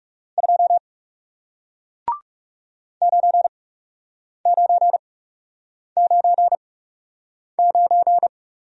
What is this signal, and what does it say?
Morse code, heard as a keyed tone, about 40 wpm, spaced as for 8 wpm, 700 Hz. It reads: 2 9999